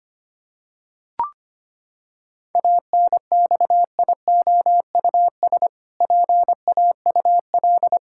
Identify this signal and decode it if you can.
Morse code, heard as a keyed tone, about 25 wpm, 700 Hz. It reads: ANXIOUS PAUL
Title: Pangram Challenge v2 - 25wpm